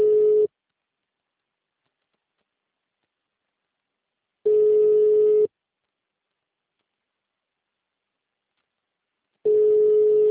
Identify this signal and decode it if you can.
Ringback tone (Europe, 425 Hz).